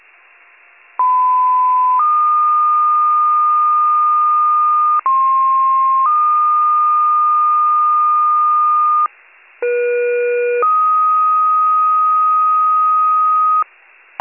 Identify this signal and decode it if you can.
Two-tone page, heard as a one-second tone, then a three-second tone, about 1010 then 1220 Hz, about 1060 then 1220 Hz, about 480 then 1250 Hz. It is heard from 1.0 to 13.6 s.